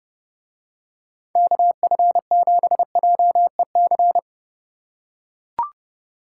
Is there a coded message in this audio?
Yes: Morse code from 1.4 to 4.2 s, heard as a keyed tone, about 30 wpm, 700 Hz. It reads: KF7JEC